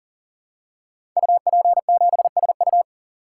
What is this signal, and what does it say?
Morse code, heard as a keyed tone, about 39 wpm, 700 Hz. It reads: UP7SU
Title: Code Groups - 5 Letter:Numbers - 40wpm